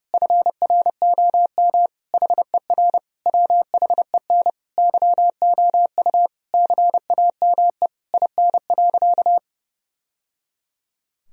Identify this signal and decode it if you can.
Morse code, heard as a keyed tone, about 32 wpm, 700 Hz. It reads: FROM HER WHEN YOU CAME IN.
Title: Sentences from Top 1000 Words - Review Part 3 - 30wpm